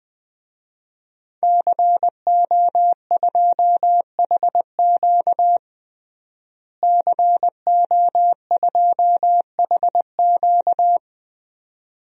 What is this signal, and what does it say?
Morse code, heard as a keyed tone, about 20 wpm, 700 Hz. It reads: CO2HQ CO2HQ
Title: DXCC-100 Call Signs - 5 Characters - Repeated 2x - 20wpm